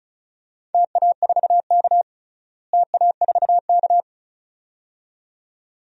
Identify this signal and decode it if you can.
Morse code, heard as a keyed tone, about 34 wpm, 700 Hz. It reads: TA4K TA4K
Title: DXCC-100 Call Signs - 4 Characters - Repeated 2x - 35wpm